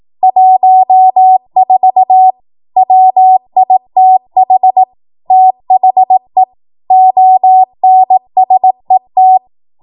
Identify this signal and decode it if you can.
Morse code, heard as a keyed tone, about 18 wpm, 750 Hz. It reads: 14 WITH THE ONSET
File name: Morse / 18WPM